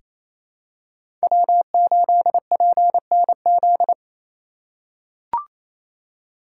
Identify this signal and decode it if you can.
Morse code, heard as a keyed tone, about 28 wpm, 700 Hz. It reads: W8PNZ
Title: USA Call Signs - 2x Character Spacing - 28wpm